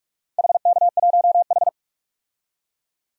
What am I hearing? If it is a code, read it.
Morse code, heard as a keyed tone, about 46 wpm, 700 Hz. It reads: HK1H
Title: DXCC-100 Call Signs - 4 Characters - 45wpm